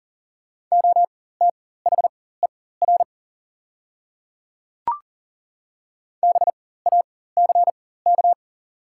Morse code, heard as a keyed tone, about 40 wpm, spaced as for 19 wpm, 700 Hz. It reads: OTHER BACK
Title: 4x Character Spacing - Top 200 Words - 40wpm